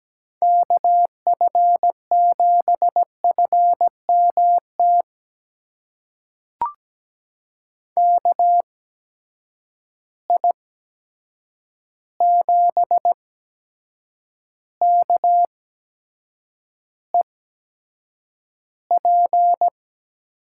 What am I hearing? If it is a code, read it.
Morse code, heard as a keyed tone, about 17 wpm, spaced as for 7 wpm, 700 Hz. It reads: KF7FMT KI7KEP
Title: USA Call Signs - 6 Characters - 8x Character Spacing - 17wpm